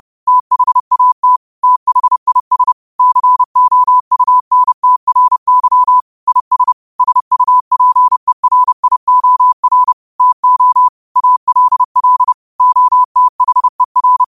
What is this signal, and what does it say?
Morse code, heard as a keyed tone, about 30 wpm, 1000 Hz. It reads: THAT THIS COUNTRY IS SUPERIOR TO ALL OTHER